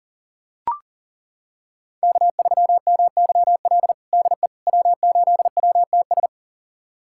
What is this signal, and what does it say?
Morse code, heard as a keyed tone, about 40 wpm, 700 Hz. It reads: K3MYL DE W8WTS
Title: USA Callsign DE Callsign - 5 Characters - 40wpm